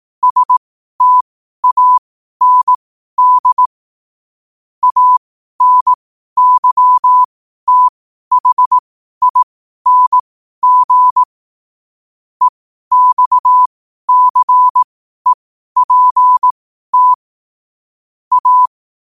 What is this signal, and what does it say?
Morse code, heard as a keyed tone, about 18 wpm, spaced as for 12 wpm, 1000 Hz. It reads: STAND ANYTHING EXCEPT A
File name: qotdfiles / 12wpm